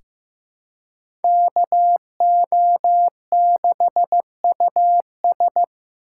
Morse code, heard as a keyed tone, about 15 wpm, 700 Hz. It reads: KO6US
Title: USA Call Signs - 2x Character Spacing - 15wpm